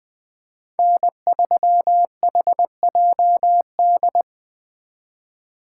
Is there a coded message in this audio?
Yes: Morse code from 0.8 to 4.2 s, heard as a keyed tone, about 21 wpm, 700 Hz. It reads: N3HJD